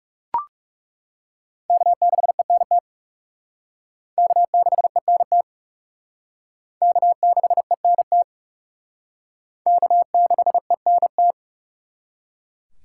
Morse code, heard as a keyed tone, about 36 wpm, 700 Hz. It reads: K6ENT K6ENT K6ENT K6ENT
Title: Speed Racing - USA 1x3 Call Signs 30wpm